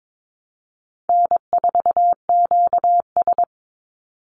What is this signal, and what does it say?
Morse code, heard as a keyed tone, about 22 wpm, 700 Hz. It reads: N4QS
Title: Easy Call Signs - 22wpm